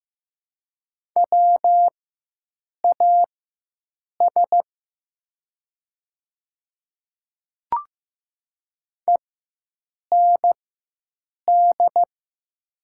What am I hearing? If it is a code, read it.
Morse code, heard as a keyed tone, about 15 wpm, spaced as for 7 wpm, 700 Hz. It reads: WAS END
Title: 4x Character Spacing - Top 200 Words - 15wpm